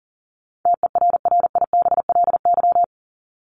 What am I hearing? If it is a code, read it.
Morse code, heard as a keyed tone, about 40 wpm, 700 Hz. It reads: TERRIBLY
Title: Top 1751-2000 Words 40wpm